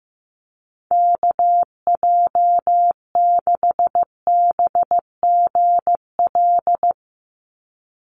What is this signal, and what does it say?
Morse code, heard as a keyed tone, about 15 wpm, 700 Hz. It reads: KJ6BGL
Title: USA Call Signs 15wpm